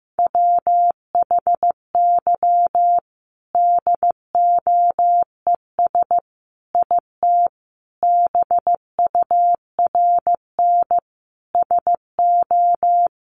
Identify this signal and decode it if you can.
Morse code, heard as a keyed tone, about 15 wpm, 700 Hz. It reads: WHY DOES IT BURN SO
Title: Sentences from Top 1300 Words 15wpm